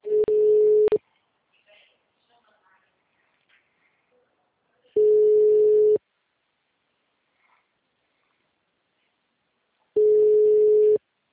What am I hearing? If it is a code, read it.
Ringback tone (Europe, 425 Hz), with speech.